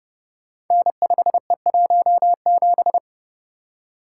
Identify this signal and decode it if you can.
Morse code, heard as a keyed tone, about 30 wpm, 700 Hz. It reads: N5E17
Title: Code Groups - 5 Letter:Numbers - 30wpm